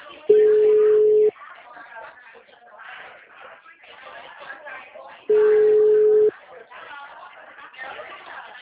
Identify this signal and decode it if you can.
Ringback tone (Europe, 425 Hz), with speech.
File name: out- -2027- - - .39665